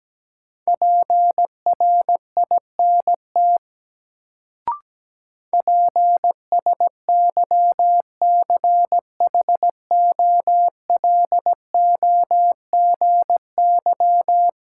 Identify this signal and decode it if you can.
Morse code, heard as a keyed tone, about 17 wpm, 700 Hz. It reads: PRINT PSYCHOLOGY
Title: Sentences from Top 1400 Words - New Word Drill Alt - 17wpm